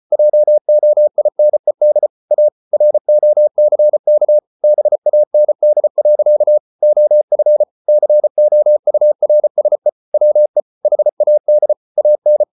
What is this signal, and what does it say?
Morse code, heard as a keyed tone, about 34 wpm, 600 Hz. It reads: JOINED A ROCK BAND. OF COURSE WE HAD AN